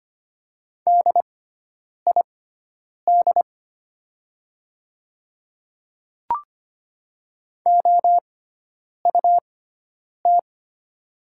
Morse code, heard as a keyed tone, about 25 wpm, spaced as for 9 wpm, 690 Hz. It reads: DID OUT
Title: 6x Character Spacing - Top 200 Words - 25wpm